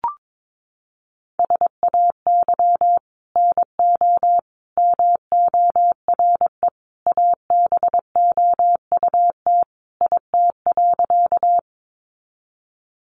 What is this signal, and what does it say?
Morse code, heard as a keyed tone, about 22 wpm, 700 Hz. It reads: SAY NO MORE ABOUT IT.
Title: Sentences from Top 200 Words 22wpm